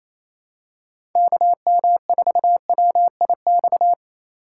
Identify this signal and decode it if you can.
Morse code, heard as a keyed tone, about 28 wpm, 700 Hz. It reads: KM4WIX
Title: USA Call Signs - 6 Characters - 3x Character Spacing - 28wpm